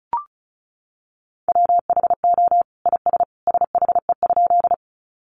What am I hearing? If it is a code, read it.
Morse code, heard as a keyed tone, about 34 wpm, 700 Hz. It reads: WHO IS SHE?
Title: Sentences from Top 100 Words 35wpm